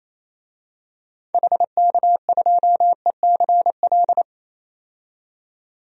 Morse code, heard as a keyed tone, about 28 wpm, 700 Hz. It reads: HK2ECL
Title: DXCC-100 Call Signs - 6 Characters - 28wpm